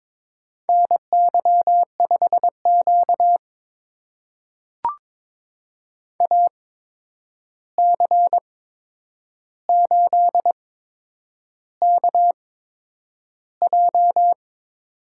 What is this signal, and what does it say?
Morse code, heard as a keyed tone, about 22 wpm, spaced as for 9 wpm, 700 Hz. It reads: NY5Q AC8KJ
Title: USA Call Signs - 8x Character Spacing - 22wpm